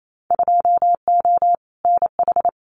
Morse code, heard as a keyed tone, about 28 wpm, 700 Hz. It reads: 2O NH
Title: Winter Field Day Exchanges 28wpm